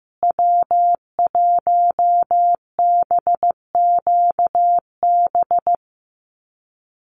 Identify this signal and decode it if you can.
Morse code, heard as a keyed tone, about 15 wpm, 700 Hz. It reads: W1BQB
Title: USA Call Signs Repeated 2x 15wpm